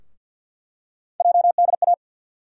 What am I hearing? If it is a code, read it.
Morse code, heard as a keyed tone, about 47 wpm, 700 Hz. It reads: JDA
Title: USA Call Sign Postfixes - 3 Letters 50wpm